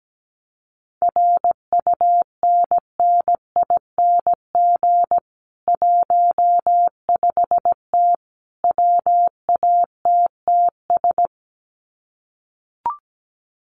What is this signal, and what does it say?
Morse code, heard as a keyed tone, about 17 wpm, 700 Hz. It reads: RUNNING 15T WATTS
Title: Common QSO Phrases 17wpm